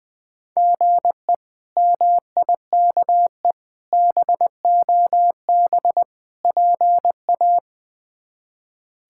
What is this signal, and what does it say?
Morse code, heard as a keyed tone, about 19 wpm, 700 Hz. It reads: GE MIKE BOB PA
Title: K1USN SST QSO Practice - 20wpm